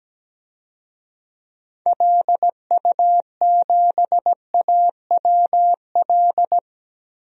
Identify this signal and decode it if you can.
Morse code, heard as a keyed tone, about 17 wpm, 700 Hz. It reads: LU7AWL
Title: DXCC-100 Call Signs - 6 Characters - 17wpm